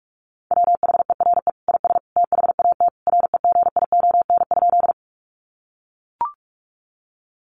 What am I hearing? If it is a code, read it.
Morse code, heard as a keyed tone, about 44 wpm, 700 Hz. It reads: WHERE IS THAT REGION?
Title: Sentences from Top 1500 Words 45wpm